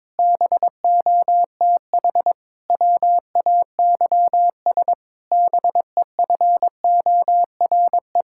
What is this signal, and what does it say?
Morse code, heard as a keyed tone, about 22 wpm, 700 Hz. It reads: BOTH WAYS BEFORE
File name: Difficult Sentences with Hints - 22wpm